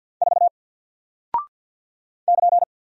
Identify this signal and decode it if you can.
Morse code, heard as a keyed tone, about 50 wpm, 700 Hz. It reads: 4 <KN>
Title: Single Character (Letter:Number:Punctuation:Prosign) Alt 50wpm